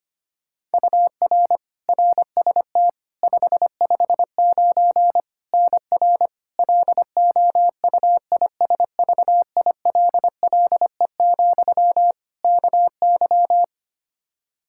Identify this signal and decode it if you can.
Morse code, heard as a keyed tone, about 25 wpm, 700 Hz. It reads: UR RST 559 NR LOUISVILLE, KY
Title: K1USN SST QSO Practice 25wpm